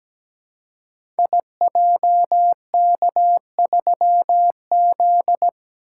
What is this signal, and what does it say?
Morse code, heard as a keyed tone, about 17 wpm, 700 Hz. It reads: IJK3Z
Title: Code Groups - 5 Letter:Numbers - 17wpm